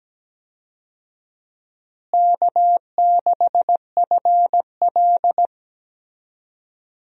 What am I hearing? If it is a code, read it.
Morse code, heard as a keyed tone, about 17 wpm, 700 Hz. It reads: K6FL
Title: USA Call Signs - 4 Characters - Repeated 3x - 17wpm